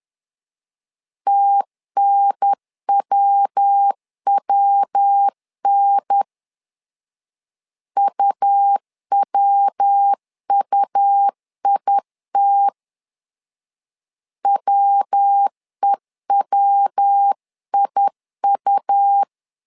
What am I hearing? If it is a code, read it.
Morse code, heard as a keyed tone, about 10 wpm, 800 Hz. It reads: TNWWN UWUIT WEWIU
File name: LEZIONI / 3 MWRO / 3 MW 10wpm 1959